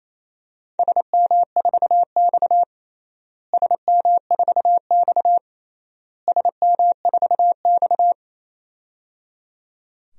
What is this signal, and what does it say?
Morse code, heard as a keyed tone, about 28 wpm, 700 Hz. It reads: SM4X SM4X SM4X
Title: DXCC-100 Call Signs - 4 Characters - Repeated 3x - 28wpm